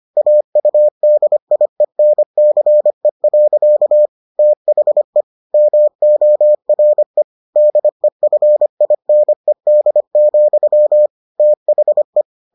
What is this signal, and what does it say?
Morse code, heard as a keyed tone, about 25 wpm, 600 Hz. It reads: AUDIENCE. THE MORE DEFINED, THE